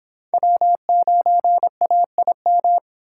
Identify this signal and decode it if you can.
Morse code, heard as a keyed tone, about 30 wpm, 700 Hz. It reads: W9AIM